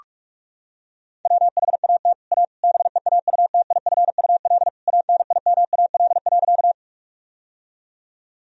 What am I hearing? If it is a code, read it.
Morse code, heard as a keyed tone, about 47 wpm, 700 Hz. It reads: WHAT A BEAUTIFUL ANIMAL.
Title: Sentences from Top 1000 Words - Review Part 1 - 45wpm